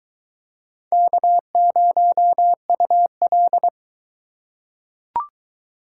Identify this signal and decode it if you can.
Morse code, heard as a keyed tone, about 21 wpm, 700 Hz. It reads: K0UL